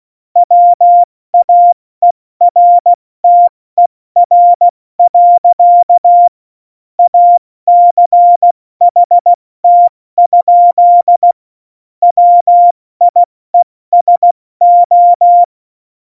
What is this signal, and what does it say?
Morse code, heard as a keyed tone, about 16 wpm, 700 Hz. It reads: WAERTER. ACHT? WIESO
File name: Morsekurs / Witze / Witz 27. Februar 2018, 70 F80